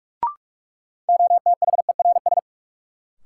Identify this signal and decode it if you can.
Morse code, heard as a keyed tone, about 46 wpm, 700 Hz. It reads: OTHERS
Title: Sentences from Top 1000 Words - Review Part 2 - New Word Drill - 45wpm